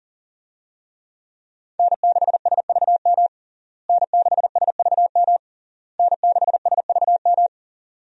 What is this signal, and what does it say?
Morse code, heard as a keyed tone, about 40 wpm, 700 Hz. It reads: N6SVM N6SVM N6SVM
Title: USA Call Signs - 5 Characters - Repeated 3x - 40wpm